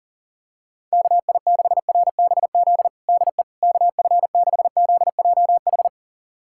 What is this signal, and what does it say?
Morse code, heard as a keyed tone, about 40 wpm, 700 Hz. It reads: KI6RBZ DE KF6ZJH
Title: USA Callsign DE Callsign - 6 Characters - 40wpm